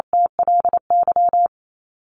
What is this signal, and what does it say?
Morse code, heard as a keyed tone, about 27 wpm, 700 Hz. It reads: TLY
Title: LY Suffix - Top 100 Words with Base Words 28wpm